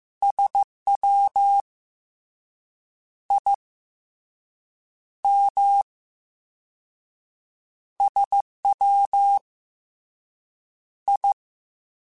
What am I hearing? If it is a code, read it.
Morse code, heard as a keyed tone, about 15 wpm, 800 Hz. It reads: SW I M SW I
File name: Misc / Misc-mp3s / Jumps @13